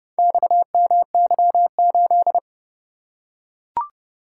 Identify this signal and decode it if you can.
Morse code, heard as a keyed tone, about 28 wpm, 700 Hz. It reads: XMY8